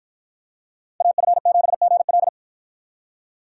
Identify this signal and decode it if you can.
Morse code, heard as a keyed tone, about 53 wpm, 700 Hz. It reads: AF7GL